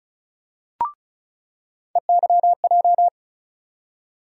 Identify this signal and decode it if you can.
Morse code, heard as a keyed tone, about 34 wpm, 700 Hz. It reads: EYJ
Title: USA Call Sign Postfixes - 3 Letters 35wpm